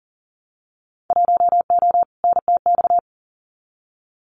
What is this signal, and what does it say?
Morse code, heard as a keyed tone, about 40 wpm, 700 Hz. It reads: 1O NTX
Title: Winter Field Day Exchanges 40wpm